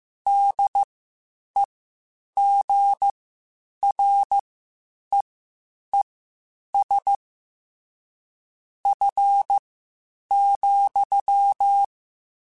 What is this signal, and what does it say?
Morse code, heard as a keyed tone, about 15 wpm, spaced as for 8 wpm, 800 Hz. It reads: DEGREES F,
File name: Spaced Characters / Spaced Characters-mp3s / Weather 3 17 sp @13